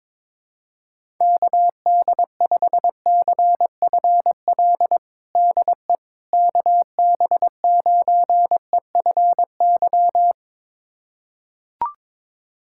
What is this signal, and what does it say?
Morse code, heard as a keyed tone, about 22 wpm, 700 Hz. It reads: KD5CFL DE KB9EFY
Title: USA Callsign DE Callsign 22wpm